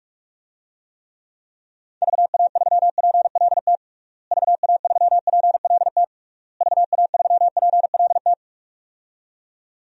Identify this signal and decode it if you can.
Morse code, heard as a keyed tone, about 46 wpm, 700 Hz. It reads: VA3PLT VA3PLT VA3PLT
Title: DXCC-100 Call Signs - 6 Characters - Repeated 3x - 45wpm